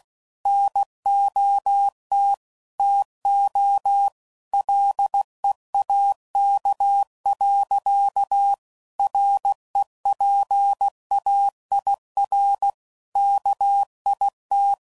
Morse code, heard as a keyed tone, about 16 wpm, 800 Hz. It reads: NOT TO LEAK. REPAIR KIT